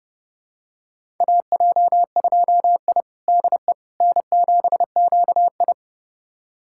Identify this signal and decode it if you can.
Morse code, heard as a keyed tone, about 30 wpm, 700 Hz. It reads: AJ2I DE N7QI
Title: USA Callsign DE Callsign - 4 Characters - 30wpm